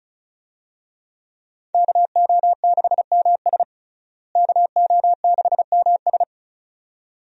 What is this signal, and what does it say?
Morse code, heard as a keyed tone, about 34 wpm, 700 Hz. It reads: KO6MS KO6MS
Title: USA Call Signs - 5 Characters - Repeated 2x - 35wpm